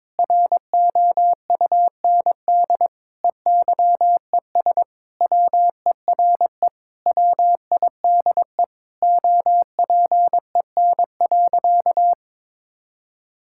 Morse code, heard as a keyed tone, about 22 wpm, 700 Hz. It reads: ROUND EYES WERE WIDE OPEN.
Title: Sentences from Top 1000 Words - Review Part 6 - 22wpm